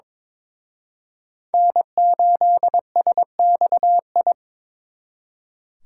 Morse code, heard as a keyed tone, about 26 wpm, 700 Hz. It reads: N8SXI
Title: Speed Racing - USA Call Signs 22wpm